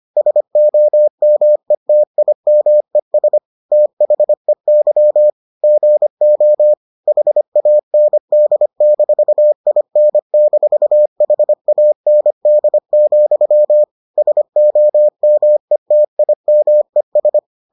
Morse code, heard as a keyed tone, about 25 wpm, 600 Hz. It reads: SOMETIMES THEY GO HAND-IN-HAND, SOMETIMES